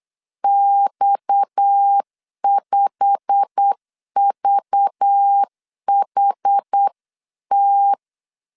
Morse code, heard as a keyed tone, about 8 wpm, 800 Hz. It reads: X5VHT
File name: 8wpm